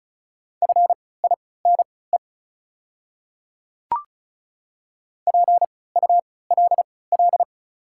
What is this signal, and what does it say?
Morse code, heard as a keyed tone, about 36 wpm, spaced as for 20 wpm, 690 Hz. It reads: FINE PULL
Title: 3x Character Spacing - Top 500 Words - 35wpm